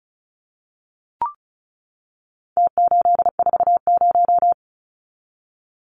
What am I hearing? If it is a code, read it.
Morse code, heard as a keyed tone, about 34 wpm, 700 Hz. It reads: T840
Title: Numbers - Contest Style - 4-Digit 35wpm